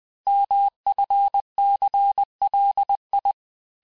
Morse code, heard as a keyed tone, about 20 wpm, 800 Hz. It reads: MFCLI
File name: cours-1584-teste-chiffre-vitesse-19mpm